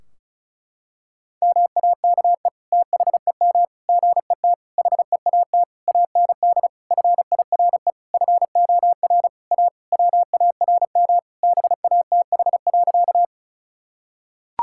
Morse code, heard as a keyed tone, about 36 wpm, 700 Hz. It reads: MAKE THEM GET HEAT AND FIRE FOR A WARM BATH.
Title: Sentences from Top 1600 Words 35wpm